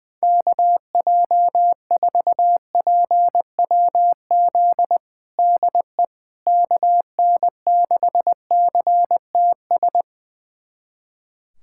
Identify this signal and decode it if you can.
Morse code, heard as a keyed tone, about 21 wpm, 700 Hz. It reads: KJ4PWZ DE KN6CTS
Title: USA Callsign DE Callsign 20wpm